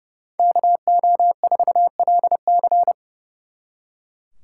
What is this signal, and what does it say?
Morse code, heard as a keyed tone, about 36 wpm, 700 Hz. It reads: KO4LC
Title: Speed Racing - USA 2x2 Call Signs 30wpm